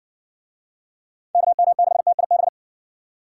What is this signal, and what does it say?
Morse code, heard as a keyed tone, about 56 wpm, 700 Hz. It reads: KM6TIB